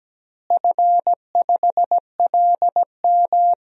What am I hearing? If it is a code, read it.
Morse code, heard as a keyed tone, about 17 wpm, 700 Hz. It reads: F5LM
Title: DXCC-100 Call Signs - 4 Characters - Repeated 3x - 17wpm